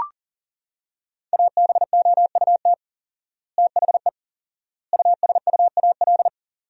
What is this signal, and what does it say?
Morse code, heard as a keyed tone, about 37 wpm, 700 Hz. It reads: ABOUT THE USUAL
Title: 4x Word-Spacing - Sentences from Top 400 Words - 40wpm